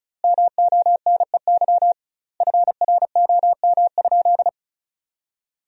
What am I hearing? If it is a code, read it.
Morse code, heard as a keyed tone, about 36 wpm, 700 Hz. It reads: MONEY FROM?
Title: 2x Word-Spacing - Sentences from Top 400 Words - 35wpm